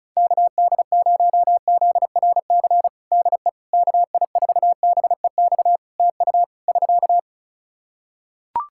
Morse code, heard as a keyed tone, about 34 wpm, 700 Hz. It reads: KD0ZRC DE KI4BEX TU <SK>
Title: Practice QSO - Exchange 4 - Ending - 35wpm